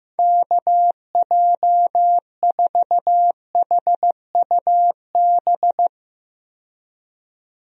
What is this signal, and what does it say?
Morse code, heard as a keyed tone, about 16 wpm, 700 Hz. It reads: KJ4HUB
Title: USA Call Signs - 6 Characters - 15wpm